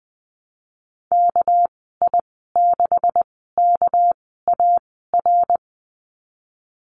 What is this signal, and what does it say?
Morse code, heard as a keyed tone, about 20 wpm, spaced as for 15 wpm, 700 Hz. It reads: KI6KAR
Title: 200 Lesson 32g - Letter k - Call Signs 20wpm